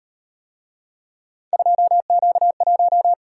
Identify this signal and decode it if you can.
Morse code, heard as a keyed tone, about 38 wpm, 700 Hz. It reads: 2Q1